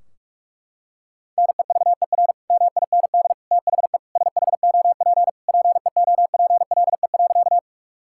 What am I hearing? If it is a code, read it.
Morse code, heard as a keyed tone, about 46 wpm, 700 Hz. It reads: NEVER MIND THE SHOP PEOPLE.
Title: Sentences from Top 1600 Words 45wpm